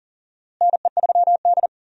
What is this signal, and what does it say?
Morse code, heard as a keyed tone, about 40 wpm, 700 Hz. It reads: NE3D
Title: USA Call Signs - 4 Characters - Repeated 2x - 40wpm